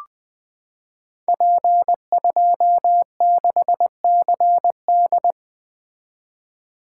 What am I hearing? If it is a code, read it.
Morse code, heard as a keyed tone, about 20 wpm, 700 Hz. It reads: P26CD
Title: Code Groups - 5 Letter:Numbers - 20wpm